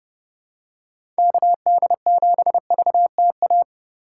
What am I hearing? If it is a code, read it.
Morse code, heard as a keyed tone, about 30 wpm, 700 Hz. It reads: KD7VTA
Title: USA Call Signs - 6 Characters - 2x Character Spacing - 30wpm